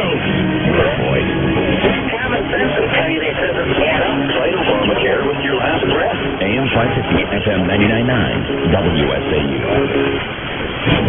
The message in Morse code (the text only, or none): none